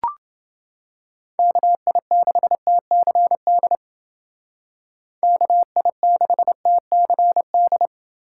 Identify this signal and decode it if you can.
Morse code, heard as a keyed tone, about 28 wpm, 700 Hz. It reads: KI6TCD KI6TCD